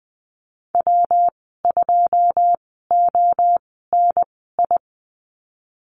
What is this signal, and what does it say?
Morse code, heard as a keyed tone, about 20 wpm, spaced as for 15 wpm, 700 Hz. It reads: W2ONI
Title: 122 Lesson 20g - Number 3 - Call Signs 20wpm